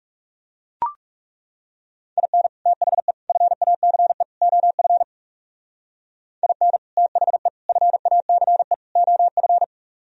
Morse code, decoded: IN THE FACE OF IN THE FACE OF

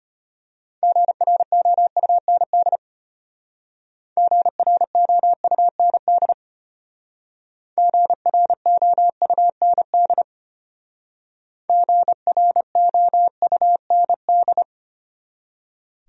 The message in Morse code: GROUND GROUND GROUND GROUND

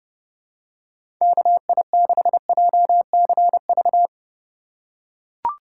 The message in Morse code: KI6JCV

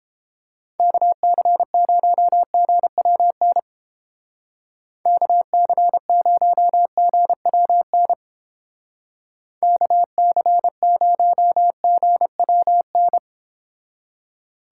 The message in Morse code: KC0GWN KC0GWN KC0GWN

